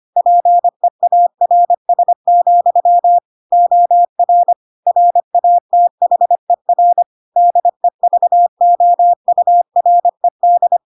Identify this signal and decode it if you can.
Morse code, heard as a keyed tone, about 25 wpm, 700 Hz. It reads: PEARS, OR RATHER DEVOURED
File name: Pinocchio / Pinocchio ch7 25 25 ik1hge